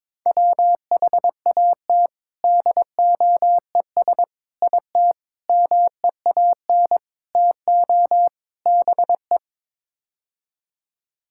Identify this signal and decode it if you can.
Morse code, heard as a keyed tone, about 22 wpm, 700 Hz. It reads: WHAT DOES IT MEAN TO BE
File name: practice / 22wpm / Sets of 6 Words - 22wpm